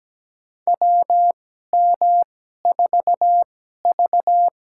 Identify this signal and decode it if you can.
Morse code, heard as a keyed tone, about 17 wpm, spaced as for 12 wpm, 700 Hz. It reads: WM4V
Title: USA Call Signs - 4 Characters - 2x Character Spacing - 17wpm